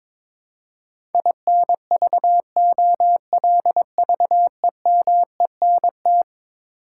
Morse code, heard as a keyed tone, about 22 wpm, 700 Hz. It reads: INVOLVEMENT
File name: practice / 22wpm / Eleven Letter Words - 22wpm